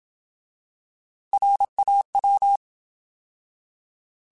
Morse code, heard as a keyed tone, about 27 wpm, 800 Hz. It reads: RAW
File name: Alphabetical / Alphabetical mp3s / Rs @25